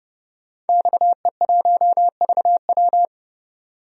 Morse code, heard as a keyed tone, about 30 wpm, 690 Hz. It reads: XE1VW
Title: DXCC-100 Call Signs - 5 Characters - 30wpm